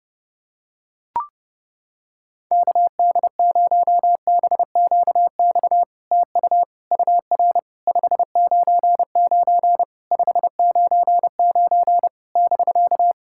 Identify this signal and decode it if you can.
Morse code, heard as a keyed tone, about 30 wpm, 700 Hz. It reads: KD0BQX TU UR 599 599 <BK>
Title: SOTA QSO Practice 30wpm